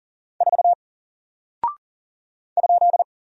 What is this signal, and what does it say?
Morse code, heard as a keyed tone, about 40 wpm, 700 Hz. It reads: 4 ?
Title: Single Character (Letter:Number:Punctuation:Prosign) 40wpm